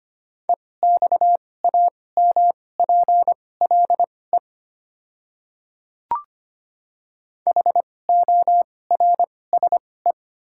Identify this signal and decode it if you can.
Morse code, heard as a keyed tone, about 25 wpm, spaced as for 18 wpm, 690 Hz. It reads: EXAMPLE HORSE
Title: 2x Character Spacing - Top 500 Words - 25wpm